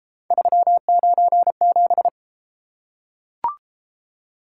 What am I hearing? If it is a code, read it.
Morse code, heard as a keyed tone, about 31 wpm, 700 Hz. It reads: 397